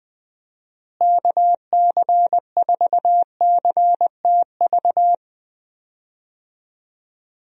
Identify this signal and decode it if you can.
Morse code, heard as a keyed tone, about 19 wpm, 700 Hz. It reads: KC4CTV
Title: USA Call Signs - 6 Characters - Repeated 2x - 20wpm